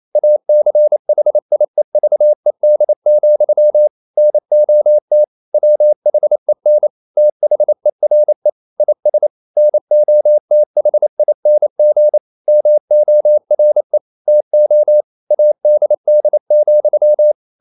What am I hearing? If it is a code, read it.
Morse code, heard as a keyed tone, about 28 wpm, 600 Hz. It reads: ACHIEVED, NOT WHEN THERE IS NOTHING MORE TO ADD,
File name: CW / Practice / EnglishQuotations0228